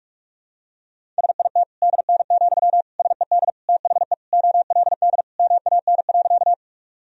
Morse code, heard as a keyed tone, about 46 wpm, 700 Hz. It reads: SIT DN, SED THE OLD MAN.
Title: Sentences from Top 500 Words and Abbreviations 45wpm